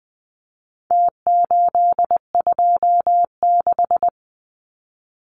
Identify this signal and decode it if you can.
Morse code, heard as a keyed tone, about 20 wpm, 700 Hz. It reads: T826
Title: Numbers - Contest Style - 4-Digit 20wpm